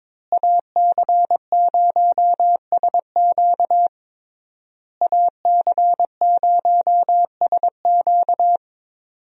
Morse code, heard as a keyed tone, about 22 wpm, 700 Hz. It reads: AC0SQ AC0SQ
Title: USA Call Signs - 5 Characters - Repeated 3x - 22wpm